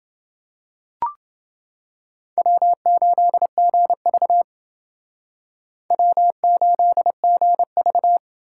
Morse code, heard as a keyed tone, about 28 wpm, 700 Hz. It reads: W8GV W8GV